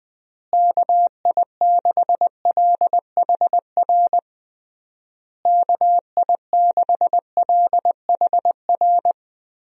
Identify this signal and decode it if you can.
Morse code, heard as a keyed tone, about 20 wpm, 700 Hz. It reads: KI6LHR KI6LHR
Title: USA Call Signs - 6 Characters - Repeated 3x - 20wpm